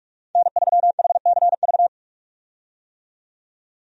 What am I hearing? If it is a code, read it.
Morse code, heard as a keyed tone, about 44 wpm, 700 Hz. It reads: N3HCV
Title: USA Call Signs - 5 Characters - 45wpm